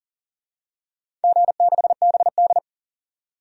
Morse code, heard as a keyed tone, about 40 wpm, 700 Hz. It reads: G6BD
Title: DXCC-100 Call Signs - 4 Characters - 40wpm